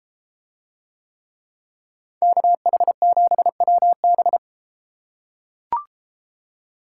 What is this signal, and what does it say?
Morse code, heard as a keyed tone, about 31 wpm, 700 Hz. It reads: KH7WB